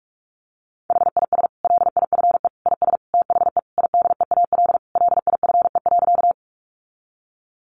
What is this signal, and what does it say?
Morse code, heard as a keyed tone, about 44 wpm, 700 Hz. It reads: HIS LIFE IS THE IDEAL LIFE.
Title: Sentences from Top 1400 Words 45wpm